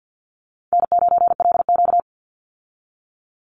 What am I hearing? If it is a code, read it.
Morse code, heard as a keyed tone, about 50 wpm, 700 Hz. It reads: N9LQ
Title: Easy Call Signs Repeated 3x - 50wpm